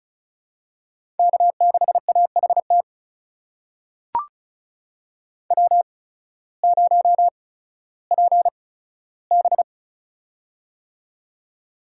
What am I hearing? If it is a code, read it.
Morse code, heard as a keyed tone, about 34 wpm, spaced as for 16 wpm, 700 Hz. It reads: K6AHT W0PB
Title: USA Call Signs - 8x Character Spacing - 35wpm